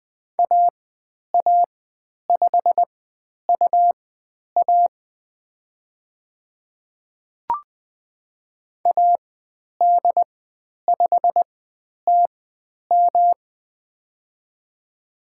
Morse code, heard as a keyed tone, about 20 wpm, spaced as for 10 wpm, 700 Hz. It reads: AA5UA AD5TM
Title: 109 Lesson 18g - Letter m - Call Signs 20wpm